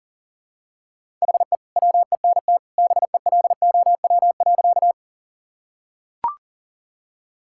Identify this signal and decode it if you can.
Morse code, heard as a keyed tone, about 37 wpm, 700 Hz. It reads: HE WENT BELOW.
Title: Sentences from Top 1000 Words - Review Part 5 - 40wpm